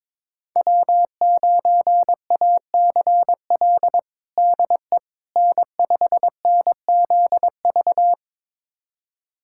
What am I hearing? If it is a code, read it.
Morse code, heard as a keyed tone, about 21 wpm, 700 Hz. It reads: W9ACL DE N5NZV
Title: USA Callsign DE Callsign - 5 Characters - 22wpm